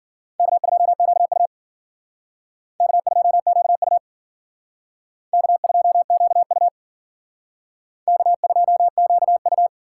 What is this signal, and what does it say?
Morse code, heard as a keyed tone, about 51 wpm, 700 Hz. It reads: K2QU K2QU K2QU K2QU